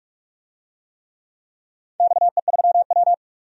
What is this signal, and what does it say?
Morse code, heard as a keyed tone, about 46 wpm, 700 Hz. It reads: XE3W
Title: DXCC-100 Call Signs - 4 Characters - 45wpm